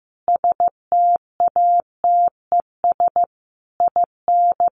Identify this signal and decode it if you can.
Morse code, heard as a keyed tone, about 15 wpm, 700 Hz. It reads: STATES IN
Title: Sets of 5 Words Encore 15wpm